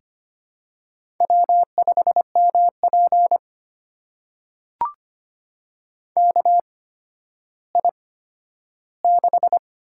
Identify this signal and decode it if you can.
Morse code, heard as a keyed tone, about 25 wpm, spaced as for 12 wpm, 700 Hz. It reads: W5MP KI6
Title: USA Call Signs - 8x Character Spacing - 25wpm